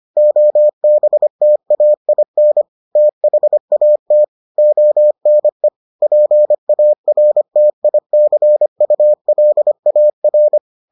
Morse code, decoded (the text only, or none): OBTAIN THAT ONE PARTICULAR